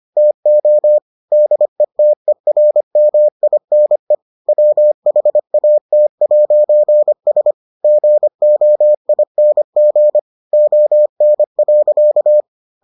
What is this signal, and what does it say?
Morse code, heard as a keyed tone, about 25 wpm, 600 Hz. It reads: TO DETERMINE WHAT'S GOING ON.